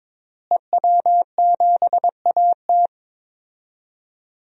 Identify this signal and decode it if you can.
Morse code, heard as a keyed tone, about 22 wpm, 700 Hz. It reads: EW7AT
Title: DXCC-100 Call Signs - 5 Characters - Repeated 3x - 22wpm